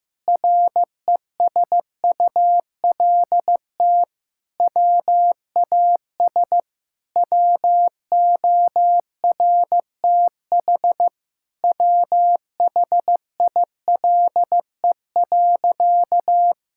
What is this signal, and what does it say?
Morse code, heard as a keyed tone, about 15 wpm, 700 Hz. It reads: RESULT WAS WORTH WHILE.
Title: Sentences from Top 1000 Words - Review Part 5 - 15wpm